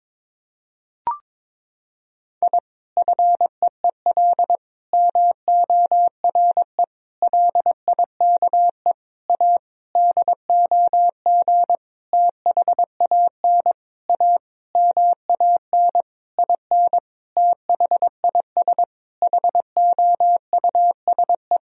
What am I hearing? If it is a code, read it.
Morse code, heard as a keyed tone, about 22 wpm, 700 Hz. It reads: I FEEL MORE LIKE A DOG THAN A MAN IN THIS HOUSE
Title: Sentences from Top 1000 Words - Review Part 2 - 22wpm